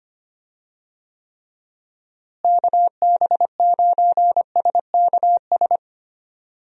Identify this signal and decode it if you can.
Morse code, heard as a keyed tone, about 25 wpm, 700 Hz. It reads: KB9SKS
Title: USA Call Signs - 6 Characters - Repeated 2x - 25wpm